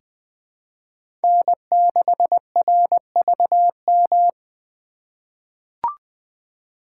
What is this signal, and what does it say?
Morse code, heard as a keyed tone, about 21 wpm, 700 Hz. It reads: N6RVM
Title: USA Call Signs - 2x Character Spacing - 20wpm